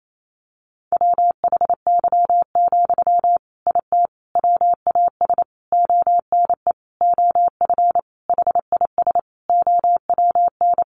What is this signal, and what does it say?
Morse code, heard as a keyed tone, about 28 wpm, 700 Hz. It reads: WHY, IT WAS ONE OF HIS OWN
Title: Sentences from Top 300 Words 28wpm